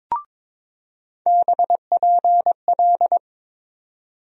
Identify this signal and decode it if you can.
Morse code, heard as a keyed tone, about 22 wpm, 700 Hz. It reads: BPL
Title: USA Call Sign Postfixes - 3 Letters 22wpm